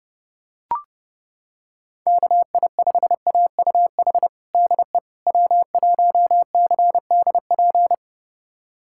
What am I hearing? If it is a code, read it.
Morse code, heard as a keyed tone, about 30 wpm, 700 Hz. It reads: KI5AUH DE W1CDP
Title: USA Callsign DE Callsign 30wpm